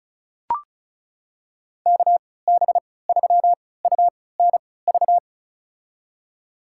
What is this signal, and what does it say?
Morse code, heard as a keyed tone, about 34 wpm, spaced as for 20 wpm, 700 Hz. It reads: KB3UNV
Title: USA Call Signs - 6 Characters - 3x Character Spacing - 35wpm